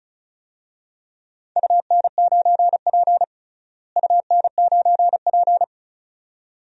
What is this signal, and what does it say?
Morse code, heard as a keyed tone, about 36 wpm, 700 Hz. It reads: UN9P UN9P
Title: DXCC-100 Call Signs - 4 Characters - Repeated 2x - 35wpm